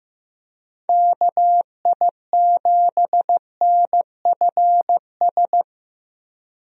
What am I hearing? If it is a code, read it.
Morse code, heard as a keyed tone, about 16 wpm, 700 Hz. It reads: KI7NFS
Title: USA Call Signs - 6 Characters - Repeated 2x - 15wpm